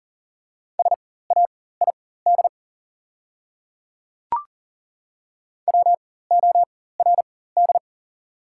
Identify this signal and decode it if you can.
Morse code, heard as a keyed tone, about 40 wpm, spaced as for 19 wpm, 690 Hz. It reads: SAID WORD
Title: 4x Character Spacing - Top 200 Words - 40wpm